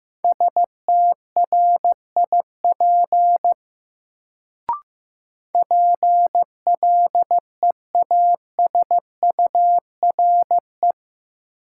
Morse code, heard as a keyed tone, about 15 wpm, 700 Hz. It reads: STRIP PLEASURE
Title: Sentences from Top 1600 Words - New Word Drill Alt - 15wpm